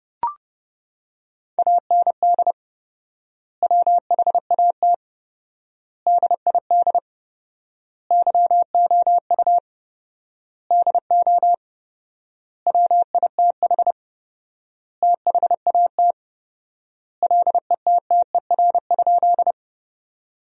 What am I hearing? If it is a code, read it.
Morse code, heard as a keyed tone, about 32 wpm, 700 Hz. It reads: AND WHAT DID YOU DO WITH THAT LETTER?
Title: 4x Word-Spacing - Sentences from Top 300 Words - 30wpm